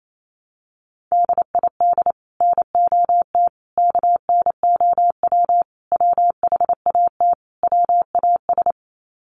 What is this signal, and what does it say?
Morse code, heard as a keyed tone, about 28 wpm, 700 Hz. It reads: DID NOT KNOW WHAT WAS
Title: Sets of 5 Words Encore2 28wpm